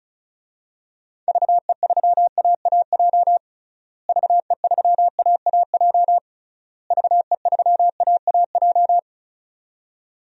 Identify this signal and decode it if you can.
Morse code, heard as a keyed tone, about 36 wpm, 700 Hz. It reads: VE3AAJ VE3AAJ VE3AAJ
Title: DXCC-100 Call Signs - 6 Characters - Repeated 3x - 35wpm